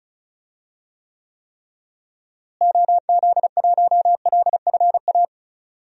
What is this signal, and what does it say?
Morse code, heard as a keyed tone, about 34 wpm, 700 Hz. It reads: OZ1LFA
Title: DXCC-100 Call Signs - 6 Characters - Repeated 3x - 35wpm